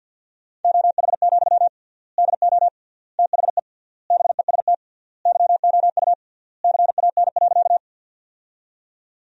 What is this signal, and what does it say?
Morse code, heard as a keyed tone, about 48 wpm, 700 Hz. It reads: OH, DO THE BEST YOU CAN.
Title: 3x Word-Spacing - Sentences from Top 500 Words - 50wpm